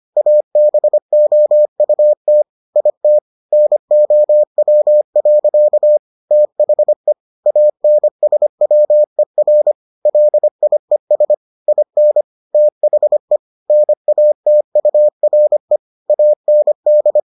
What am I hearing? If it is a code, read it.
Morse code, heard as a keyed tone, about 25 wpm, 600 Hz. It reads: ABOUT IT NOW. THE ANSWER LIES IN THE NATURE AND